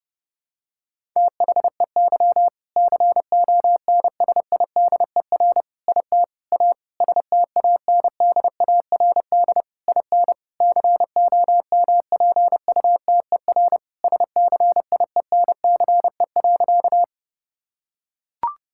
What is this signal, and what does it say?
Morse code, heard as a keyed tone, about 30 wpm, 700 Hz. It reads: THEY CONSIDER IT A STANDARD IN COMPUTER SCIENCE.
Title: Sentences from Top 1000 Words - Review Part 6 - 30wpm